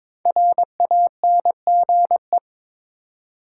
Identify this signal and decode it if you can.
Morse code, heard as a keyed tone, about 22 wpm, 700 Hz. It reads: RANGE
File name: 5 Letter Words - First 3 Letters Given - 22wpm